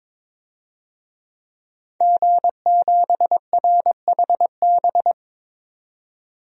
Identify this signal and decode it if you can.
Morse code, heard as a keyed tone, about 21 wpm, 700 Hz. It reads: G7RHB
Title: DXCC-100 Call Signs - 5 Characters - 22wpm